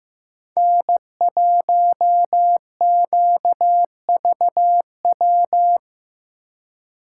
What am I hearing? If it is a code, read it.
Morse code, heard as a keyed tone, about 17 wpm, 700 Hz. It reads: N1QVW
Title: Speed Racing - USA 1x3 Call Signs 15wpm